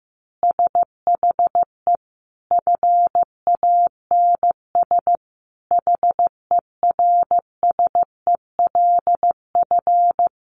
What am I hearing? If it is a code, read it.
Morse code, heard as a keyed tone, about 14 wpm, 700 Hz. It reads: SHE FANS HERSELF
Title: Easy Phrases 15wpm